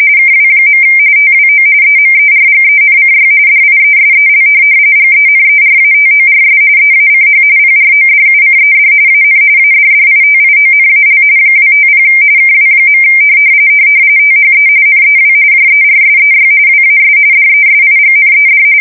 RTTY, about 45.45 baud. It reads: KW. TWO EMERGENCY DIESEL GENERATORS EACH PRODUCED AN ADDITIONAL 250 KW. THE NEW JERSEY WAS DESIGNED
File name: fsk170 / USS-New-Jersey